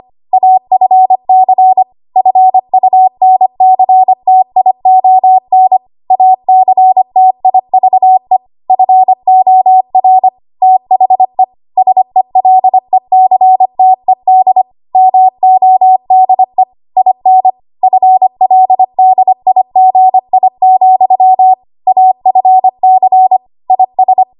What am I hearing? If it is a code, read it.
Morse code, heard as a keyed tone, about 25 wpm, 750 Hz. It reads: AFC FUNCTION ACTIVE FOR THE SELECTED MODE IN FLDIGI, AFC IS
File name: Archive / 25 WPM / 25WPM